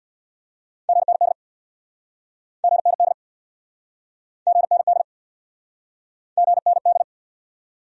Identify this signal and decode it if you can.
Morse code, heard as a keyed tone, about 69 wpm, 700 Hz. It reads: GND GND GND GND